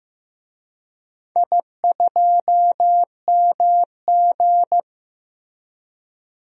Morse code, heard as a keyed tone, about 16 wpm, 700 Hz. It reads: I2MG
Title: DXCC-100 Call Signs - 4 Characters - Repeated 2x - 15wpm